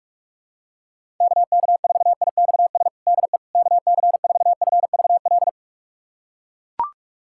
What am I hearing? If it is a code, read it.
Morse code, heard as a keyed tone, about 42 wpm, 700 Hz. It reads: KK4IXS DE KC4FVL
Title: USA Callsign DE Callsign - 6 Characters - 45wpm